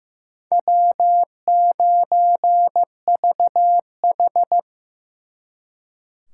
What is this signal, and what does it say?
Morse code, heard as a keyed tone, about 18 wpm, 700 Hz. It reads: W9VH
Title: Speed Racing - USA Call Signs 15wpm